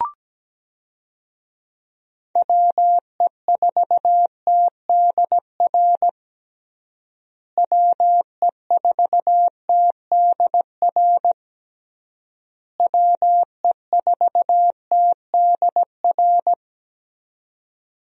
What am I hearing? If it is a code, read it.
Morse code, heard as a keyed tone, about 17 wpm, 700 Hz. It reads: WE4TDR WE4TDR WE4TDR
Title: USA Call Signs - 6 Characters - Repeated 3x - 17wpm